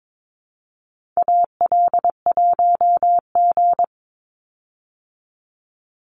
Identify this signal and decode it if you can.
Morse code, heard as a keyed tone, about 22 wpm, 700 Hz. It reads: AL1G
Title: Call Signs Repeated 2x - 22wpm